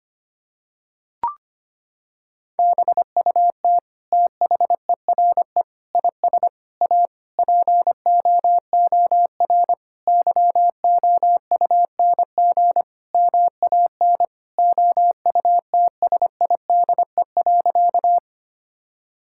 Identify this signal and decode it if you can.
Morse code, heard as a keyed tone, about 25 wpm, 700 Hz. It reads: BUT THERE IS A POOR YOUNG MAN OUTSIDE.
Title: Sentences from Top 1000 Words - Review Part 6 - 25wpm